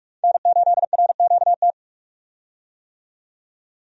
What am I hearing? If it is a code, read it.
Morse code, heard as a keyed tone, about 43 wpm, 700 Hz. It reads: N8RQT
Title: USA Call Signs - 5 Characters - 45wpm